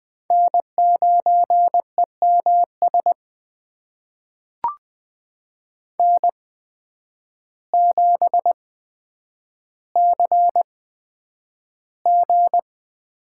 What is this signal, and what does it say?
Morse code, heard as a keyed tone, about 20 wpm, spaced as for 9 wpm, 700 Hz. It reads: N9EMS N7CG
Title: USA Call Signs - 8x Character Spacing - 20wpm